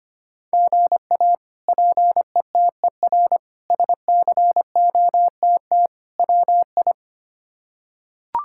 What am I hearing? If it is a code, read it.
Morse code, heard as a keyed tone, about 25 wpm, 700 Hz. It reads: GA PETER SCOTT WI
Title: K1USN SST QSO Practice - 25wpm